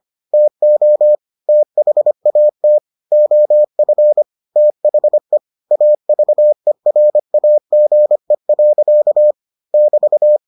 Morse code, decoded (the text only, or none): TO THAT OF THE AVERAGE. <BT>